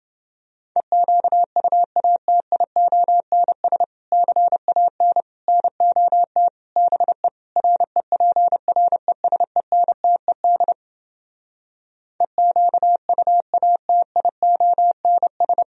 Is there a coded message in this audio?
Yes: Morse code from 0.8 to 15.6 s, heard as a keyed tone, about 30 wpm, 700 Hz. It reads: EQUATIONS CAN NOT BE REPRESENTED EQUATIONS